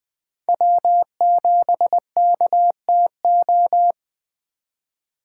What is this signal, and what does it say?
Morse code, heard as a keyed tone, about 20 wpm, 700 Hz. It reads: W7KTO
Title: USA Call Signs - 5 Characters - 4x Character Spacing - 20wpm